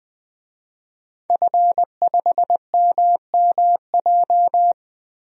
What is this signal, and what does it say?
Morse code, heard as a keyed tone, about 20 wpm, 700 Hz. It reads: F5MMJ
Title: DXCC-100 Call Signs - 5 Characters - 20wpm